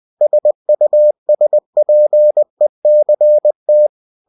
Morse code, heard as a keyed tone, about 20 wpm, 600 Hz. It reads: SUSPECT
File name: cw-books / winnie-the-pooh / winnie-20wpm-page-0007